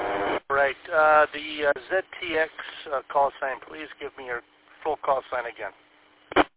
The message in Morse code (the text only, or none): none